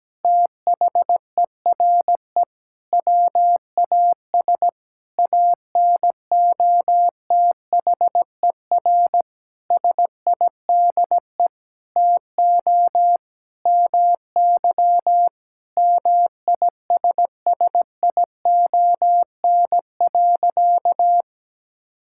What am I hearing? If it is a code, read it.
Morse code, heard as a keyed tone, about 17 wpm, 700 Hz. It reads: THERE WAS ANOTHER SIDE TO MY MISSION.
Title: Sentences from Top 1600 Words 17wpm